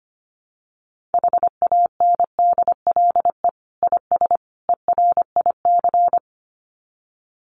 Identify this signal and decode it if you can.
Morse code, heard as a keyed tone, about 25 wpm, 700 Hz. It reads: HANDLE IS ERIC
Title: Common QSO Phrases 25wpm